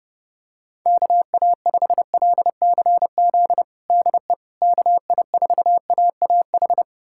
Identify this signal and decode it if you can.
Morse code, heard as a keyed tone, about 29 wpm, 700 Hz. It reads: KA5LCZ DE KI4AAH
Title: USA Callsign DE Callsign - 6 Characters - 30wpm